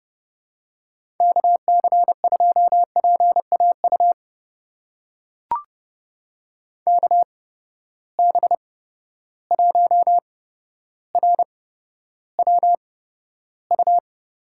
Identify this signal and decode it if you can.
Morse code, heard as a keyed tone, about 30 wpm, spaced as for 13 wpm, 700 Hz. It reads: KC2PAU KB1RWU
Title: USA Call Signs - 6 Characters - 8x Character Spacing - 30wpm